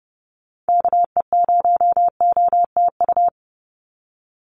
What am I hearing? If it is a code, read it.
Morse code, heard as a keyed tone, about 30 wpm, 700 Hz. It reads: KE0OTU
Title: USA Call Signs 30wpm